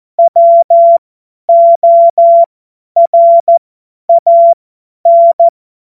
Morse code, heard as a keyed tone, about 14 wpm, spaced as for 10 wpm, 680 Hz. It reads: WORAN
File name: Morsekurs / Witze / Witz 24 Juni 2025 50 F70